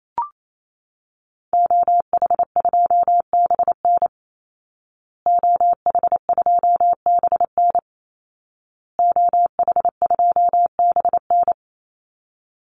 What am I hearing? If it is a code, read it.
Morse code, heard as a keyed tone, about 28 wpm, 700 Hz. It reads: OH2BN OH2BN OH2BN
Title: Call Signs Repeated 2x - 28wpm